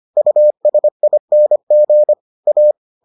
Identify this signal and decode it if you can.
Morse code, heard as a keyed tone, about 25 wpm, 600 Hz. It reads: USING A